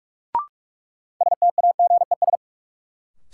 Morse code, heard as a keyed tone, about 43 wpm, 700 Hz. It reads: STAGES
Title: Sets of 5 Words Encore - New Word Drill - 45wpm